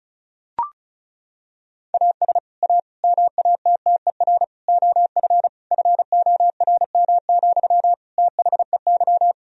Morse code, heard as a keyed tone, about 36 wpm, 700 Hz. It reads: AS A MATTER OF FORM, THEY
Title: Sentences from Top 1000 Words - Review Part 1 - 35wpm